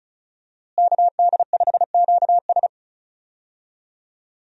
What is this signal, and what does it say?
Morse code, heard as a keyed tone, about 34 wpm, 700 Hz. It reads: KD5QS
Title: USA Call Signs - 5 Characters - Repeated 2x - 35wpm